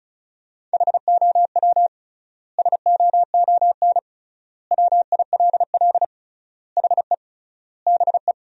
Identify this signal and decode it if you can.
Morse code, heard as a keyed tone, about 36 wpm, 700 Hz. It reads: HOW SOON WILL HE BE
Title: 3x Word-Spacing - Sentences from Top 400 Words - 35wpm